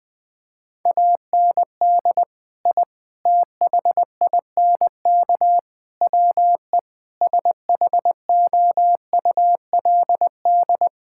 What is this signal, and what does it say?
Morse code, heard as a keyed tone, about 20 wpm, 700 Hz. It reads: AND I THINK WE SHOULD
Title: Sentences with 4-Word Phrases - Spoken Beginning - 20wpm